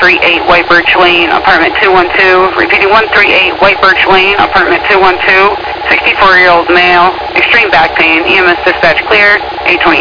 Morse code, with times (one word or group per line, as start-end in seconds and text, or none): none